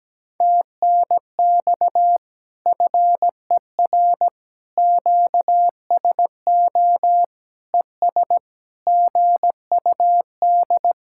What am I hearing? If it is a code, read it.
Morse code, heard as a keyed tone, about 17 wpm, 700 Hz. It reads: TNX FER QSO ES GUD
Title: Speed-Racing Common QSO Phrases - 17wpm